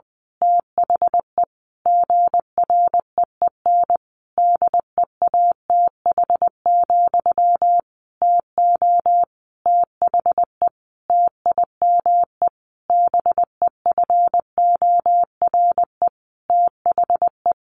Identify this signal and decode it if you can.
Morse code, heard as a keyed tone, about 20 wpm, 700 Hz. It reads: THE GREEN DEATH, TO THE TIME BEFORE THE
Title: A Sound Of Thunder 20wpm